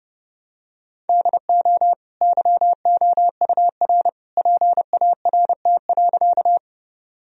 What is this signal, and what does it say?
Morse code, heard as a keyed tone, about 30 wpm, 700 Hz. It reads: DO YOUR PART.
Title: Sentences from Top 1000 Words - Review Part 1 - 30wpm